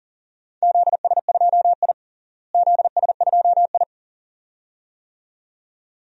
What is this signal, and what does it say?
Morse code, heard as a keyed tone, about 40 wpm, 700 Hz. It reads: ZS2I ZS2I
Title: DXCC-100 Call Signs - 4 Characters - Repeated 2x - 40wpm